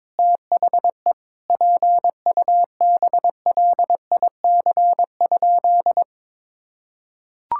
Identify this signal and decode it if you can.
Morse code, heard as a keyed tone, about 22 wpm, 700 Hz. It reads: THE PUBLIC?
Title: Sentences from Top 1000 Words - Review Part 5 - 22wpm